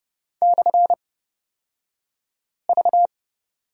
Morse code, heard as a keyed tone, about 30 wpm, spaced as for 5 wpm, 700 Hz. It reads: /V
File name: Lesson 37d - Letter x - Two Characters 30wpm